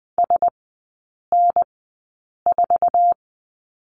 Morse code, heard as a keyed tone, about 20 wpm, spaced as for 8 wpm, 700 Hz. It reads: SN4
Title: 047 Lesson 9e - Number 4 - Three Characters 20wpm